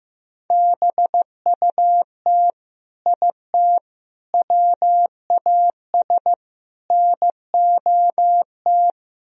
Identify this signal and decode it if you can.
Morse code, heard as a keyed tone, about 16 wpm, 700 Hz. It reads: BUT IT WAS NOT
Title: Speed-Racing - Sets of 4 Words 15wpm